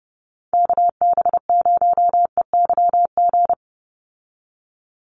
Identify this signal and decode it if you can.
Morse code, heard as a keyed tone, about 30 wpm, 700 Hz. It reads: KB0EYG
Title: USA Call Signs 30wpm